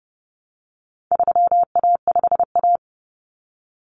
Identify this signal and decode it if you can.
Morse code, heard as a keyed tone, about 30 wpm, 700 Hz. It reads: 3A5A
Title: Common Cut Numbers - 4-Digit 30wpm